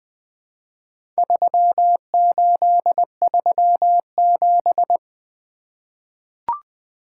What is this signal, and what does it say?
Morse code, heard as a keyed tone, about 20 wpm, 700 Hz. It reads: 3837
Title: Four Letter-Number Combinations - 20wpm